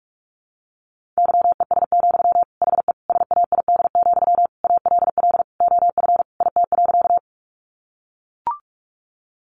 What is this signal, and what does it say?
Morse code, heard as a keyed tone, about 44 wpm, 700 Hz. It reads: YES, HE SAID, ALL OF IT.
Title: Sentences from Top 500 Words 45wpm